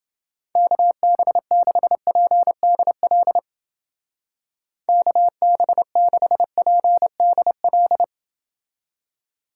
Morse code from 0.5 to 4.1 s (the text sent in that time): KB6PDL